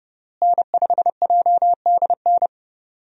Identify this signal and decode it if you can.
Morse code, heard as a keyed tone, about 30 wpm, 700 Hz. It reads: N5JDN
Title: USA Call Signs - 5 Characters - Repeated 3x - 30wpm